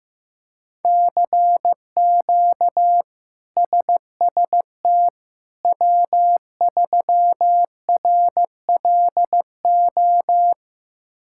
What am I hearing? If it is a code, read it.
Morse code, heard as a keyed tone, about 15 wpm, 700 Hz. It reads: CQ SST W3RLO
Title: K1USN SST QSO Practice - 15wpm